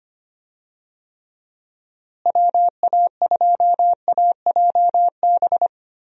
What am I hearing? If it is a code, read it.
Morse code, heard as a keyed tone, about 25 wpm, 700 Hz. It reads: WA2AJB
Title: USA Call Signs - 6 Characters - Repeated 3x - 25wpm